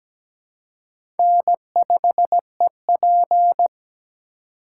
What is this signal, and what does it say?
Morse code, heard as a keyed tone, about 17 wpm, 700 Hz. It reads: N5EP
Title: USA Call Signs - 4 Characters - 2x Character Spacing - 17wpm